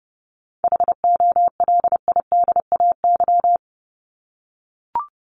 Morse code, heard as a keyed tone, about 30 wpm, 700 Hz. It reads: HOLIDAY
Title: Top 1501-1750 Words 30wpm